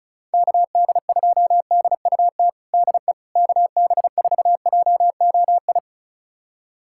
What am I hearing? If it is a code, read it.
Morse code, heard as a keyed tone, about 36 wpm, 690 Hz. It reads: KD2DUT DE KB4JOI
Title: USA Callsign DE Callsign - 6 Characters - 35wpm